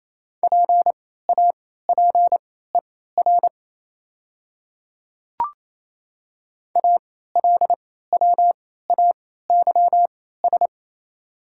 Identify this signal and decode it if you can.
Morse code, heard as a keyed tone, about 28 wpm, spaced as for 16 wpm, 700 Hz. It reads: PAPER ALWAYS
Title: 3x Character Spacing - Top 300 Words - 28wpm